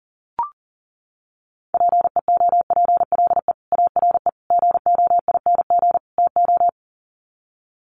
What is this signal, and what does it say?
Morse code, heard as a keyed tone, about 40 wpm, 700 Hz. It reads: PEOPLE ARE GOING TO
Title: Sets of 4 Words 40wpm